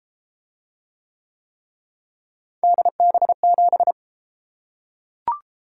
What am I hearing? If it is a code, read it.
Morse code, heard as a keyed tone, about 31 wpm, 700 Hz. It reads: DB7